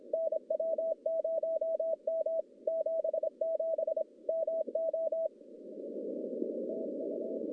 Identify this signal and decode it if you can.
Morse code, heard as a keyed tone, about 26 wpm, 640 Hz. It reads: NW0M 77 MO TU
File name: contests / CW-SS-k5tr- -1100